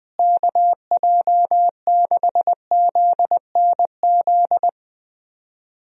Morse code, heard as a keyed tone, about 20 wpm, 700 Hz. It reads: KJ6ZNZ
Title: USA Call Signs - 6 Characters - Repeated 3x - 20wpm